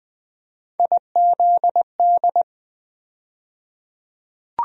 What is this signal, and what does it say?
Morse code, heard as a keyed tone, about 20 wpm, 700 Hz. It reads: IZD
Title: USA Call Sign Postfixes - 3 Letters 20wpm